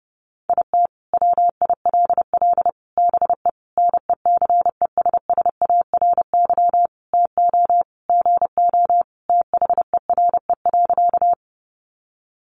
Morse code, heard as a keyed tone, about 30 wpm, 700 Hz. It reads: IT WILL BE NECESSARY TO GO THERE.
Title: Sentences from Top 800 Words 30wpm